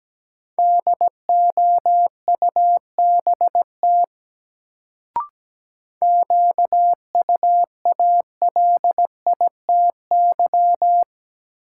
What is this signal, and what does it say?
Morse code, heard as a keyed tone, about 17 wpm, 700 Hz. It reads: DOUBT QUALITY
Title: Sentences from Top 1500 Words - New Word Drill Alt - 17wpm